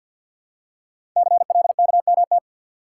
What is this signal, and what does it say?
Morse code, heard as a keyed tone, about 50 wpm, 700 Hz. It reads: CLKMT
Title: Code Groups - 5 Letters - 50wpm